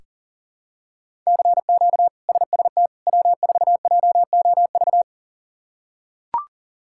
Morse code, heard as a keyed tone, about 40 wpm, 700 Hz. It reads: CQ SST W4JOV
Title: K1USN SST QSO Practice - 40wpm